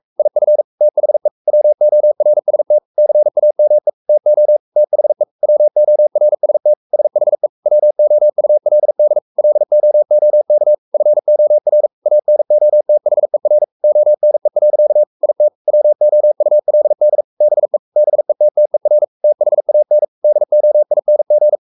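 Morse code, heard as a keyed tone, about 44 wpm, 600 Hz. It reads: IF THE WORST CAME TO THE WORST SHE WOULD LOOK FOR ANOTHER ONE. IT WOULD BE BETTER THAN DOING
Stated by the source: Morse code at 43 wpm sending Ken Follett's novel Never in English, from CW med Gnister.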